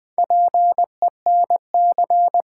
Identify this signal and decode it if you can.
Morse code, heard as a keyed tone, about 20 wpm, 700 Hz. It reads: PENC
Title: Build A Word - 2 Letters At A Time - 6-Letter Words - 20wpm